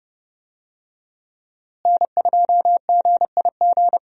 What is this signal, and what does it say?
Morse code, heard as a keyed tone, about 31 wpm, 700 Hz. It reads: N2GIG
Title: USA Call Signs - 5 Characters - Repeated 3x - 30wpm